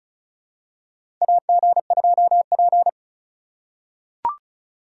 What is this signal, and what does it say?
Morse code, heard as a keyed tone, about 36 wpm, 700 Hz. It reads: AG2P
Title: USA Call Signs - 2x Character Spacing - 35wpm